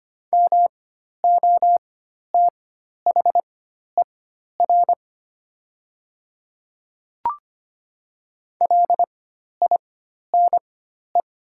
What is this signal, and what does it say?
Morse code, heard as a keyed tone, about 25 wpm, spaced as for 12 wpm, 700 Hz. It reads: MOTHER LINE
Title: 4x Character Spacing - Top 200 Words - 25wpm